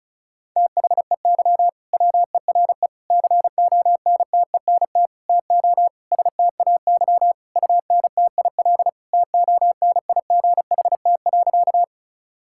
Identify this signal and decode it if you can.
Morse code, heard as a keyed tone, about 36 wpm, 700 Hz. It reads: THEY WERE CONTENT TO STAY UNTIL TONIGHT.
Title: Sentences from Top 1000 Words - Review Part 6 - 35wpm